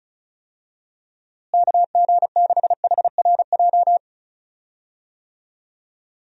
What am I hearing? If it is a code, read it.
Morse code, heard as a keyed tone, about 34 wpm, 700 Hz. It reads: KG6HRJ
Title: USA Call Signs - 6 Characters - Repeated 3x - 35wpm